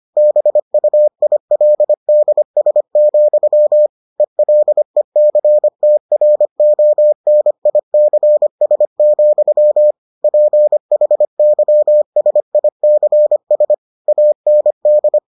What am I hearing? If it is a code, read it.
Morse code, heard as a keyed tone, about 25 wpm, 600 Hz. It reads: BUILDS, ELECTRONICS, PHYSICS AND